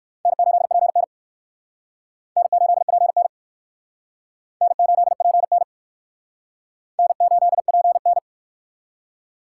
Morse code, heard as a keyed tone, about 60 wpm, 700 Hz. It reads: N8PN N8PN N8PN N8PN